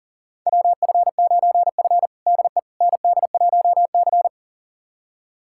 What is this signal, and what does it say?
Morse code, heard as a keyed tone, about 40 wpm, 700 Hz. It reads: WF9F DE ND1C
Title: USA Callsign DE Callsign - 4 Characters - 40wpm